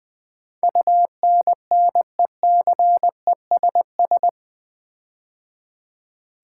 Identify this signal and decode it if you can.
Morse code, heard as a keyed tone, about 20 wpm, 700 Hz. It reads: UNNECESS
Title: Build A Word - 4 Letters At A Time - 11-Letter Words - 20wpm